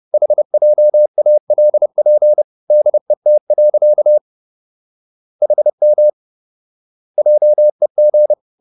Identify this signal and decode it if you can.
Morse code, heard as a keyed tone, about 30 wpm, 600 Hz. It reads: HJALP DET. HM JEG
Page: Ferie 30 wpm